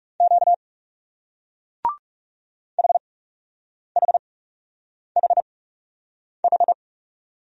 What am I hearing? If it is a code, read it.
Morse code, heard as a keyed tone, about 38 wpm, spaced as for 8 wpm, 700 Hz. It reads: Q HHHH